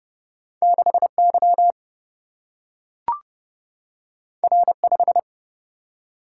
Morse code, heard as a keyed tone, about 30 wpm, 700 Hz. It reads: 6Y R5
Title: Single-Letter Single-Number 30wpm